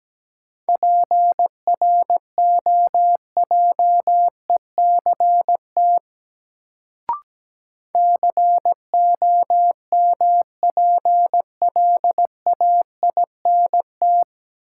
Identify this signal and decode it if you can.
Morse code, heard as a keyed tone, about 17 wpm, 700 Hz. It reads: PROJECT COMPLAINT
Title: Sentences from Top 1400 Words - New Word Drill Alt - 17wpm